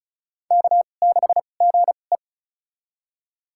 Morse code, heard as a keyed tone, about 36 wpm, spaced as for 25 wpm, 700 Hz. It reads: K6GE
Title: USA Call Signs - 2x Character Spacing - 35wpm